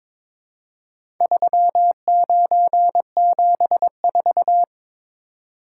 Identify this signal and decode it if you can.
Morse code, heard as a keyed tone, about 22 wpm, 700 Hz. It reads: 3974
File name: practice / 22wpm / Four Letter-Number Combinations - 22wpm